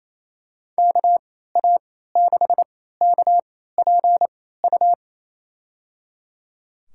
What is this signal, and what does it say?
Morse code, heard as a keyed tone, about 28 wpm, spaced as for 16 wpm, 700 Hz. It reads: KA6KPU
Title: USA Call Signs - 3x Character Spacing - 28wpm